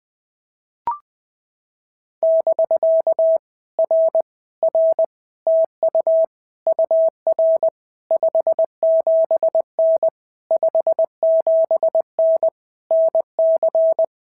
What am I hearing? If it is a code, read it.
Morse code, heard as a keyed tone, about 20 wpm, 650 Hz. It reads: <BK> R R TU UR 57N 57N NC
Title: POTA QSO Practice - 20wpm